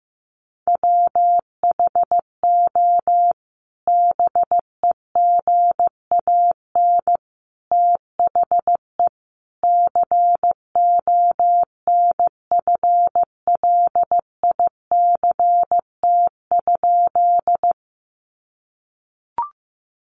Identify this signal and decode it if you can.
Morse code, heard as a keyed tone, about 15 wpm, 700 Hz. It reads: WHO BEGAN THE CONFLICT?
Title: Sentences from Top 1300 Words 15wpm